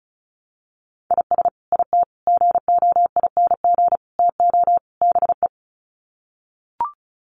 Morse code, heard as a keyed tone, about 34 wpm, 700 Hz. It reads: IS IT GOING TO BE
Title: Sets of 5 Words Encore2 35wpm